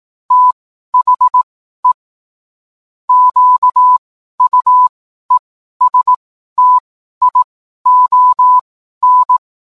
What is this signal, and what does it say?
Morse code, heard as a keyed tone, about 18 wpm, spaced as for 12 wpm, 1000 Hz. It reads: THE QUESTION